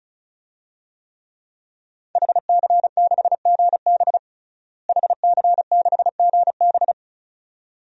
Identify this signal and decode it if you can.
Morse code, heard as a keyed tone, about 34 wpm, 700 Hz. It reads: HC6GB HC6GB
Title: DXCC-100 Call Signs - 5 Characters - Repeated 2x - 35wpm